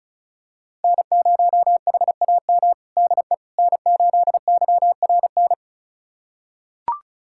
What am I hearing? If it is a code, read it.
Morse code, heard as a keyed tone, about 34 wpm, 700 Hz. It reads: N0HAM DE N8YRN
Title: USA Callsign DE Callsign - 5 Characters - 35wpm